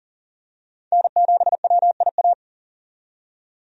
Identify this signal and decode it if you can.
Morse code, heard as a keyed tone, about 40 wpm, 700 Hz. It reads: N7WIA
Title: USA Call Signs - 5 Characters - 40wpm